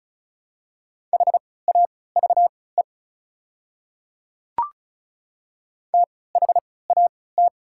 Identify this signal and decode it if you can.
Morse code, heard as a keyed tone, about 34 wpm, spaced as for 20 wpm, 690 Hz. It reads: HAVE THAT
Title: 3x Character Spacing - Top 100 Words - 35wpm